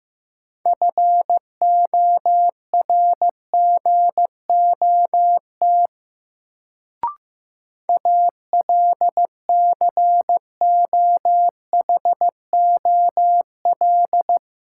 Morse code, heard as a keyed tone, about 15 wpm, 700 Hz. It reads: FORGOT ALCOHOL
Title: Sentences from Top 1400 Words - New Word Drill Alt - 15wpm